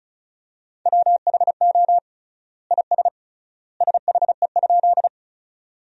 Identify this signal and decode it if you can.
Morse code, heard as a keyed tone, about 36 wpm, 700 Hz. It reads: WHO IS SHE?
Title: 3x Word-Spacing - Sentences from Top 100 Words - 35wpm